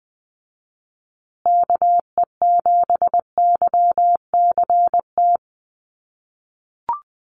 Morse code, heard as a keyed tone, about 20 wpm, 700 Hz. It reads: KE7YCT
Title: USA Call Signs Repeated 2x 20wpm